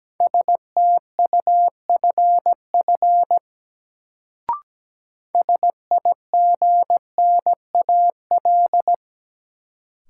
Morse code, heard as a keyed tone, about 17 wpm, 700 Hz. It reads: STUFF SIGNAL
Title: Sentences from Top 1400 Words - New Word Drill - 17wpm